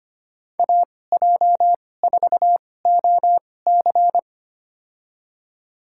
Morse code, heard as a keyed tone, about 25 wpm, spaced as for 18 wpm, 700 Hz. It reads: AJ4OC
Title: USA Call Signs - 5 Characters - 2x Character Spacing - 25wpm